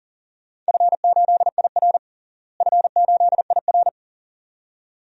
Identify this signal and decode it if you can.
Morse code, heard as a keyed tone, about 40 wpm, 700 Hz. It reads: F8IR F8IR
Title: DXCC-100 Call Signs - 4 Characters - Repeated 2x - 40wpm